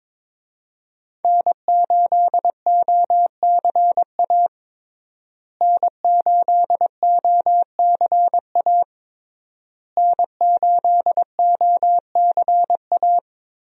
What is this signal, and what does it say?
Morse code, heard as a keyed tone, about 22 wpm, 700 Hz. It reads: N8OCA N8OCA N8OCA
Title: USA Call Signs - 5 Characters - Repeated 3x - 22wpm